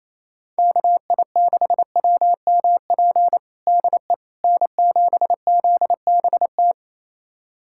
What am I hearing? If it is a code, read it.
Morse code, heard as a keyed tone, about 28 wpm, 700 Hz. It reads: KI6WMP DE N7ZBT
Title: USA Callsign DE Callsign 28wpm